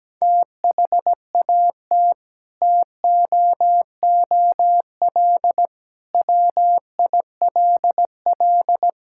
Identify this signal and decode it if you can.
Morse code, heard as a keyed tone, about 17 wpm, 700 Hz. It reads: THAT TOOL WILL
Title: Sentences from Top 1000 Words - Review Part 6 - 17wpm